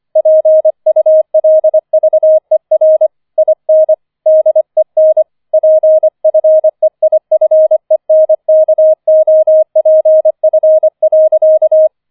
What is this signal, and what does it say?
Morse code, heard as a keyed tone, about 25 wpm, 630 Hz. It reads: PULVER IN DEN PFEIFENKOPF.
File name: 004 Vierter Streich 25wpm